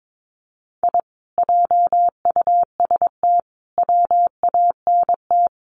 Morse code, heard as a keyed tone, about 22 wpm, 700 Hz. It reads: I JUST WANT
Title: Sets of 5 Words Encore 22wpm